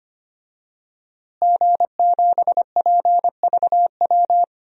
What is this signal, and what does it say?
Morse code, heard as a keyed tone, about 25 wpm, 700 Hz. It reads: G7PVW
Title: DXCC-100 Call Signs - 5 Characters - Repeated 3x - 25wpm